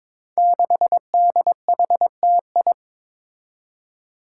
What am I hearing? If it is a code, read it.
Morse code, heard as a keyed tone, about 22 wpm, 700 Hz. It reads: 6DHTI
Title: Five Challenging Letter-Number Combinations 22wpm